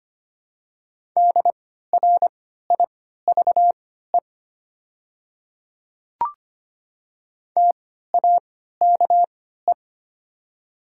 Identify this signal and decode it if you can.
Morse code, heard as a keyed tone, about 25 wpm, spaced as for 14 wpm, 700 Hz. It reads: DRIVE TAKE
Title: 3x Character Spacing - Top 500 Words - 25wpm